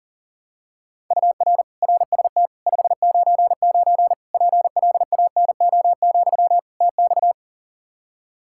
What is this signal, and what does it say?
Morse code, heard as a keyed tone, about 40 wpm, 700 Hz. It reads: UR RST 599 PLANO, TX
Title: K1USN SST QSO Practice 40wpm